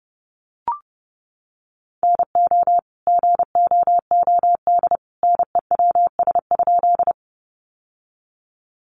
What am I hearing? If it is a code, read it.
Morse code, heard as a keyed tone, about 30 wpm, 700 Hz. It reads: NO GOOD NEWS?
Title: Sentences from Top 800 Words 30wpm